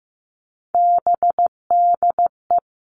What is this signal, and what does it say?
Morse code, heard as a keyed tone, about 15 wpm, 700 Hz. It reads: BDE
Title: Three Challenging Letter-Number Combinations 15wpm